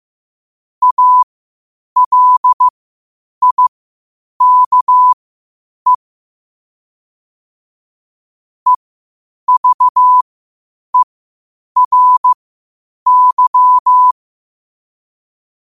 Morse code, heard as a keyed tone, about 15 wpm, spaced as for 8 wpm, 1000 Hz. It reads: ALIKE EVERY